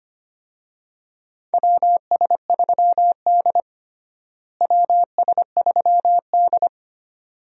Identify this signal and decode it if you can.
Morse code, heard as a keyed tone, about 25 wpm, 700 Hz. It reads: WS3D WS3D
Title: USA Call Signs - 4 Characters - Repeated 2x - 25wpm